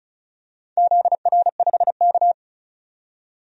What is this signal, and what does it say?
Morse code, heard as a keyed tone, about 36 wpm, 700 Hz. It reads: ZR5K
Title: DXCC-100 Call Signs - 4 Characters - 35wpm